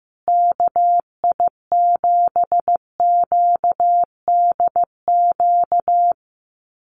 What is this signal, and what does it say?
Morse code, heard as a keyed tone, about 15 wpm, 700 Hz. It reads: KI7QDQ
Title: USA Call Signs Repeated 2x 15wpm